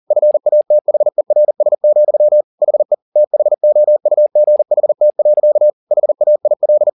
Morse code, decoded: FATHERS, HE THOUGHT. HAIL

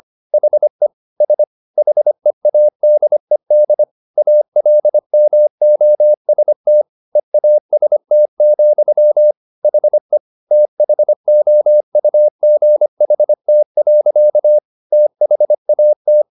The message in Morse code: HE S HEADED ALMOST EAST, HE THOUGHT. THAT

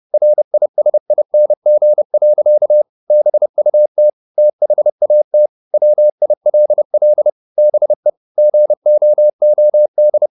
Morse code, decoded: RISING. BUT THAT WILL BE GOOD

